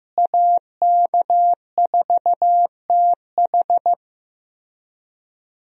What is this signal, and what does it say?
Morse code, heard as a keyed tone, about 15 wpm, 700 Hz. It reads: AK4TH
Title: USA Call Signs - 5 Characters - Repeated 3x - 15wpm